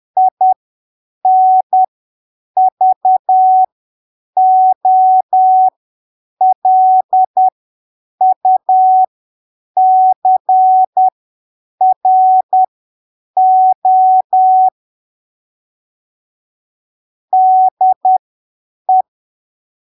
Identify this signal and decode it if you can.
Morse code, heard as a keyed tone, about 10 wpm, spaced as for 7 wpm, 750 Hz. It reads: INVOLUCRO DE
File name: arquivos / texto / texto 11 10ppm